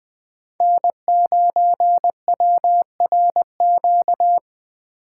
Morse code, heard as a keyed tone, about 20 wpm, 700 Hz. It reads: N 9 W R Q
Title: USA Call Signs - 5 Characters - 8x Character Spacing - 20wpm